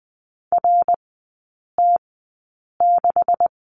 Morse code, heard as a keyed tone, about 20 wpm, spaced as for 8 wpm, 700 Hz. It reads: RT6
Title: 127 Lesson 21e - Number 6 - Three Characters 20wpm